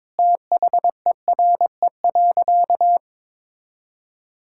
Morse code, heard as a keyed tone, about 21 wpm, 700 Hz. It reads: THERE.
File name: practice / 22wpm / Sentences with 4-Word Phrases - 22wpm